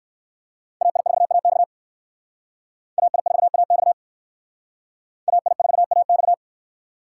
Morse code, heard as a keyed tone, about 67 wpm, 700 Hz. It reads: AI4AX AI4AX AI4AX